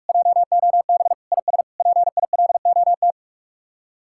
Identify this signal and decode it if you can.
Morse code, heard as a keyed tone, about 48 wpm, 700 Hz. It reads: JOB IS PILOT